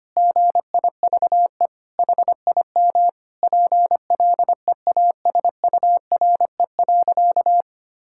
Morse code, heard as a keyed tone, about 25 wpm, 700 Hz. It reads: GIVE HIM PLEASURE.
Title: Sentences from Top 1600 Words 25wpm